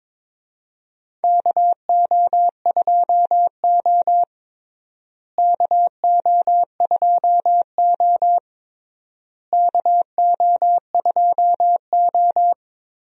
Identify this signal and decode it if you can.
Morse code, heard as a keyed tone, about 23 wpm, 700 Hz. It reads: KO2O KO2O KO2O
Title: USA Call Signs - 4 Characters - Repeated 3x - 22wpm